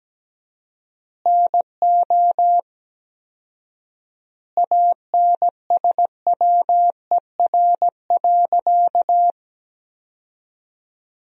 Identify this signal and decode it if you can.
Morse code, heard as a keyed tone, about 17 wpm, 700 Hz. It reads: NO ANSWER.
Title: 4x Word-Spacing - Sentences from Top 500 Words - 17wpm